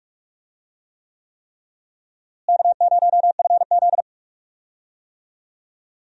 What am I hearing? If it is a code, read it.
Morse code, heard as a keyed tone, about 42 wpm, 700 Hz. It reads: K0FZ
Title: USA Call Signs - 4 Characters - Repeated 3x - 45wpm